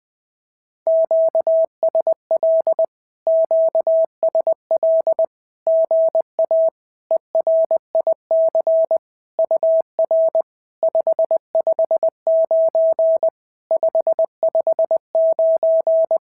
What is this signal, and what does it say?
Morse code, heard as a keyed tone, about 20 wpm, 650 Hz. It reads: QSL QSL GA ERIC UR 559 559
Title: SOTA QSO Practice 20wpm